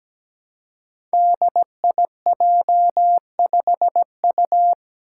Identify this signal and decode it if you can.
Morse code, heard as a keyed tone, about 17 wpm, 700 Hz. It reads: DIJ5U
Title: Five Challenging Letter-Number Combinations 17wpm